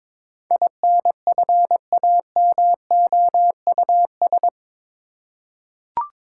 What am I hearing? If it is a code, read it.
Morse code, heard as a keyed tone, about 22 wpm, 700 Hz. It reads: INFAMOUS
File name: practice / 22wpm / Eight Letter Words - 22wpm